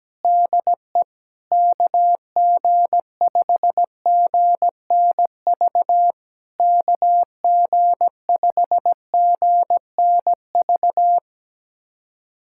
Morse code, heard as a keyed tone, about 17 wpm, 700 Hz. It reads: DE KG5GNV KG5GNV
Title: SOTA QSO Practice 17wpm